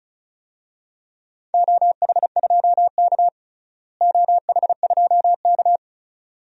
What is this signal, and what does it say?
Morse code, heard as a keyed tone, about 36 wpm, 700 Hz. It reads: OH2K OH2K
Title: DXCC-100 Call Signs - 4 Characters - Repeated 2x - 35wpm